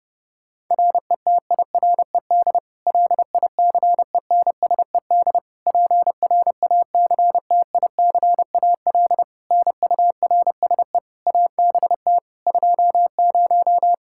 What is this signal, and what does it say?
Morse code, heard as a keyed tone, about 32 wpm, 700 Hz. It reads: RETIRED LICENSED PRACTICAL NURSE ABT 20
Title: Practice QSO - Exchange 3 - More Info - 30wpm